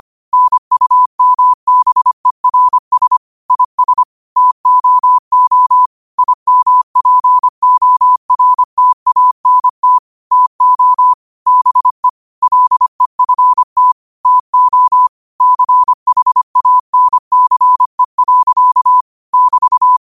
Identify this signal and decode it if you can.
Morse code, heard as a keyed tone, about 25 wpm, 1000 Hz. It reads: NUMBERS IS TOO IMPORTANT TO BE LEFT TO CHANCE. <BT>
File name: qotdfiles / 25wpm